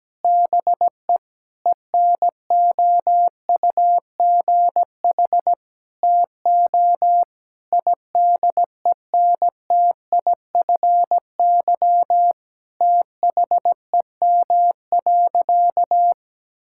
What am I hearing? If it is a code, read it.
Morse code, heard as a keyed tone, about 17 wpm, 700 Hz. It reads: BE ENOUGH TO IDENTIFY THEM.
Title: Sentences from Top 1000 Words - Review Part 6 - 17wpm